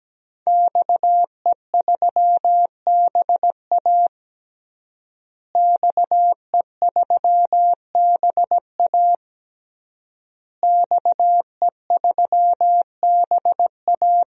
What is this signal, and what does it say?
Morse code, heard as a keyed tone, about 17 wpm, 700 Hz. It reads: XE3BA XE3BA XE3BA
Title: DXCC-100 Call Signs - 5 Characters - Repeated 3x - 17wpm